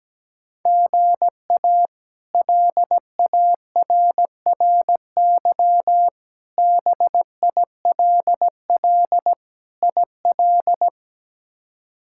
Morse code, heard as a keyed tone, about 17 wpm, 700 Hz. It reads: GA LARRY BILL IL
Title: K1USN SST QSO Practice - 17wpm